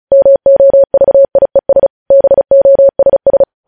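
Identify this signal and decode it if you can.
Morse code, heard as a keyed tone, about 36 wpm, 550 Hz. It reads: MOVIES BOSS
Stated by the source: Morse code practice reading news headlines at 35 words per minute.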